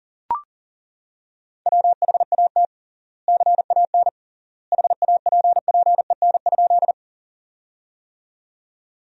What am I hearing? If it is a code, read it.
Morse code, heard as a keyed tone, about 40 wpm, 700 Hz. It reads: WHAT CAN HAPPEN?
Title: 3x Word-Spacing - Sentences from Top 400 Words - 40wpm